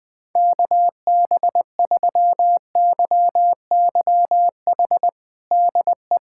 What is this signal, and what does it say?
Morse code, heard as a keyed tone, about 20 wpm, 700 Hz. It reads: KB3YYH DE
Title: USA Callsign DE Callsign - 6 Characters - 20wpm